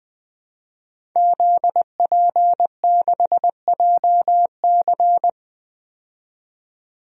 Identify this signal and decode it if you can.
Morse code, heard as a keyed tone, about 21 wpm, 700 Hz. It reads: ZP6JC
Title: DXCC-100 Call Signs - 5 Characters - 20wpm